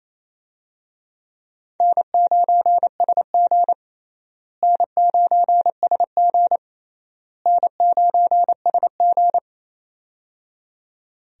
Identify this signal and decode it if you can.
Morse code, heard as a keyed tone, about 28 wpm, 700 Hz. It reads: N9SG N9SG N9SG
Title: USA Call Signs - 4 Characters - Repeated 3x - 28wpm